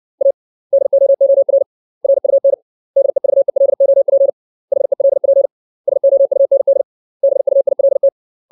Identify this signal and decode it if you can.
Morse code, heard as a keyed tone, about 60 wpm, 550 Hz. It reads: A DOOR RUN DEVELOP HELP SOUND BUILT
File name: 632 words / 632 Words 60WPM